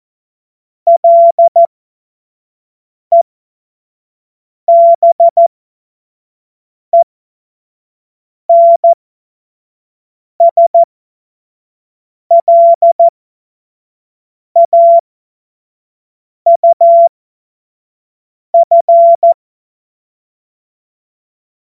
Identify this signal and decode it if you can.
Morse code, heard as a keyed tone, about 14 wpm, spaced as for 5 wpm, 680 Hz. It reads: LEBENSLAUF